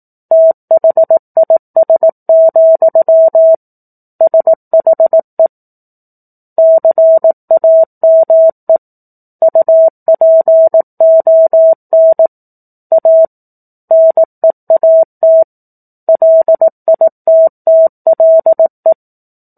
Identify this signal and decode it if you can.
Morse code, heard as a keyed tone, about 18 wpm, 650 Hz. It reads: THIS, SHE CAME UPON A NEAT LITTLE
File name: 04-Chapter-04 18 WPM